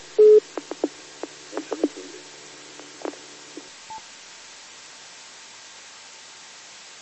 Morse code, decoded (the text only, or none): none